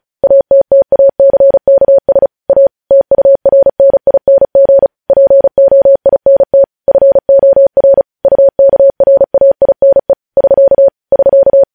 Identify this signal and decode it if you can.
Morse code, heard as a keyed tone, about 36 wpm, 550 Hz. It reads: ATTACKS A TURNING POINT FOR UKRAINE <SK> <SK>
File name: podcast / News headlines-1x-35x35